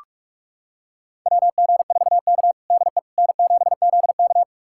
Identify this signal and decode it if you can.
Morse code, heard as a keyed tone, about 42 wpm, 700 Hz. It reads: WG4K DE N7ZK
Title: USA Callsign DE Callsign - 4 Characters - 45wpm